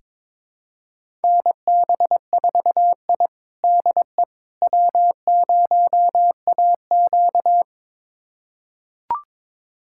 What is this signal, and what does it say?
Morse code, heard as a keyed tone, about 22 wpm, 700 Hz. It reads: NB4I DE W0AQ
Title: USA Callsign DE Callsign - 4 Characters - 22wpm